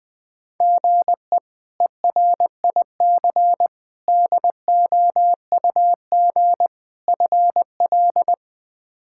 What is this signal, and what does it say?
Morse code, heard as a keyed tone, about 20 wpm, 700 Hz. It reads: GE ERIC DOUG FL
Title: K1USN SST QSO Practice - 20wpm